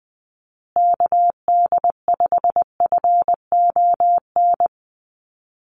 Morse code, heard as a keyed tone, about 20 wpm, 700 Hz. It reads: KD5FON
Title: USA Call Signs Repeated 3x 20wpm